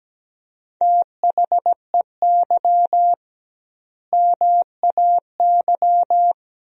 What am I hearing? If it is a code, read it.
Morse code, heard as a keyed tone, about 17 wpm, 700 Hz. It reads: THEY MAY
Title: 2x Word-Spacing - Sentences from Top 300 Words - 17wpm